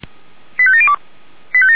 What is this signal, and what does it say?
Five-tone selective call (ZVEI), heard as quick runs of five tones, tones of about 70 ms. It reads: 85702